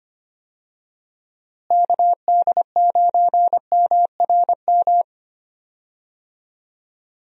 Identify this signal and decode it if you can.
Morse code, heard as a keyed tone, about 25 wpm, 700 Hz. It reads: KD9MRM
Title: USA Call Signs - 6 Characters - Repeated 2x - 25wpm